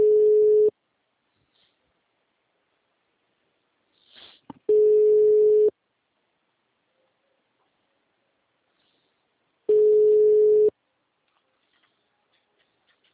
Ringback tone (Europe, 425 Hz).
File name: out- -2033- - - .83781